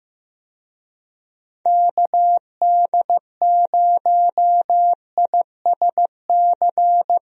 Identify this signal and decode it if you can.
Morse code, heard as a keyed tone, about 15 wpm, 700 Hz. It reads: KD0ISC
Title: USA Call Signs - 6 Characters - Repeated 3x - 15wpm